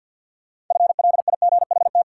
Morse code, heard as a keyed tone, about 52 wpm, 700 Hz. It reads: FLIGHT
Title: Six Letter Words - 50wpm